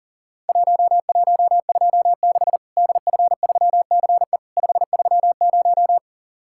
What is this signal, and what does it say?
Morse code, heard as a keyed tone, about 37 wpm, 700 Hz. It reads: 1126 DF3CE 530
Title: Worked All Europe DX Contest - QTC Exchange - 40wpm